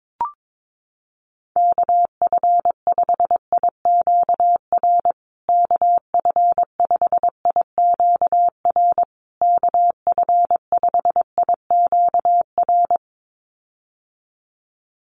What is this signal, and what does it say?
Morse code, heard as a keyed tone, about 22 wpm, 700 Hz. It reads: KF5IQR KF5IQR KF5IQR
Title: USA Call Signs Repeated 3x 22wpm